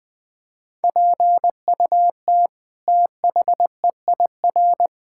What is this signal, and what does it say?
Morse code, heard as a keyed tone, about 20 wpm, 700 Hz. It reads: PUT THEIR
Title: Sets of 8 Words - 20wpm